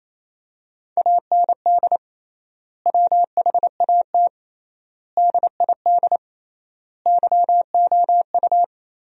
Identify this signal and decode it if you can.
Morse code, heard as a keyed tone, about 28 wpm, 700 Hz. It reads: AND WHAT DID YOU
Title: 3x Word-Spacing - Sentences from Top 100 Words - 28wpm